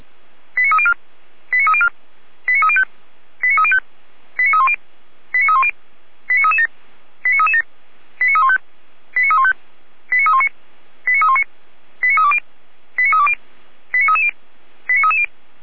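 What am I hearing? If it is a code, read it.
Five-tone selective call (ZVEI), heard as quick runs of five tones, tones of about 70 ms. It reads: 89304 89304 89305 89305 89310 89310 89337 89337 89315 89315 89319 89319 89320 89320 89330 89330
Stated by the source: ZVEI five-tone selective call, code 89270, received by SDR.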